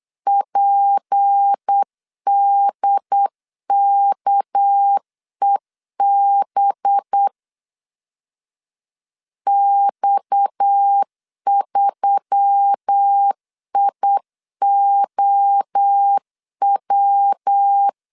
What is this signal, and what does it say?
Morse code, heard as a keyed tone, about 8 wpm, 800 Hz. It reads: PDKEB X3IOW
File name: LEZIONI / 6 PKX0 / 6 PKX 8wpm 3041